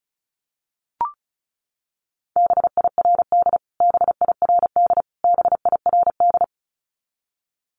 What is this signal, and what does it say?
Morse code, heard as a keyed tone, about 34 wpm, 700 Hz. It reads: BIRD BIRD BIRD
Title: Top 500 Words Repeated 35wpm